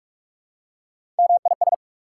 Morse code, heard as a keyed tone, about 44 wpm, 690 Hz. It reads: MIS
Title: Prefixes and Suffixes 45wpm